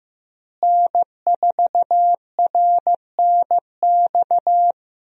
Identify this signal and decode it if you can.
Morse code, heard as a keyed tone, about 15 wpm, 700 Hz. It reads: N4RNX
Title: USA Call Signs - 5 Characters - Repeated 2x - 15wpm